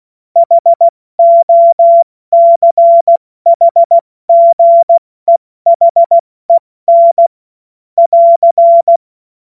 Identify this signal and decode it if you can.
Morse code, heard as a keyed tone, about 16 wpm, 680 Hz. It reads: HOCHGEHEN <AR>
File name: Witz-28-Oktober-2025-70-F80